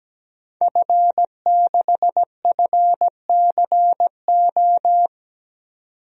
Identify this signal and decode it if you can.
Morse code, heard as a keyed tone, about 17 wpm, 700 Hz. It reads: F6FCO
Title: DXCC-100 Call Signs - 5 Characters - 17wpm